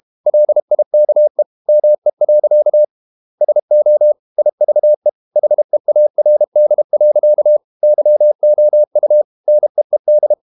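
Morse code, heard as a keyed tone, about 32 wpm, 600 Hz. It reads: LIKE ME. SO IVE HEARD. YOU NEED
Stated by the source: Morse code practice at 32 wpm from CW med Gnister, reading Ken Follett's novel Never in English.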